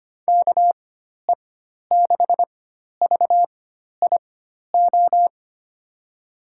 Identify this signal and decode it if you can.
Morse code, heard as a keyed tone, about 25 wpm, spaced as for 12 wpm, 700 Hz. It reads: KE6VIO
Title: USA Call Signs - 6 Characters - 4x Character Spacing - 25wpm